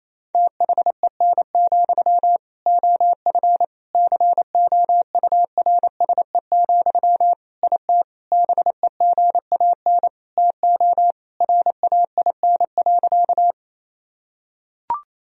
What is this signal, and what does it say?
Morse code, heard as a keyed tone, about 28 wpm, 700 Hz. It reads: THEN, OF COURSE, IT BEGAN TO RAIN.
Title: Sentences from Top 1000 Words - Review Part 1 - 28wpm